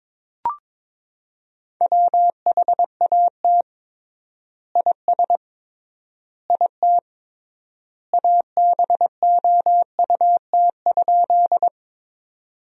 Morse code, heard as a keyed tone, about 22 wpm, 700 Hz. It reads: WHAT IS IT ABOUT?
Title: 3x Word-Spacing - Sentences from Top 100 Words - 22wpm